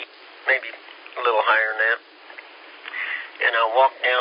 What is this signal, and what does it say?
Emergency Alert System (SAME) tones, heard as harsh buzzing bursts, with speech. It is heard from 0.4 to 3.3 s.